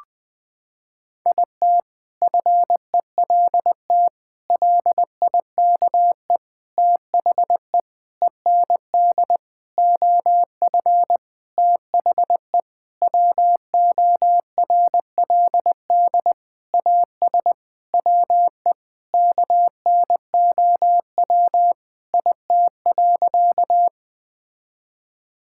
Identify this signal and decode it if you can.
Morse code, heard as a keyed tone, about 20 wpm, 700 Hz. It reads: IT FELT LIKE THE END OF THE WORLD AS WE KNOW IT.
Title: Sentences with 8-Word Phrases - 20wpm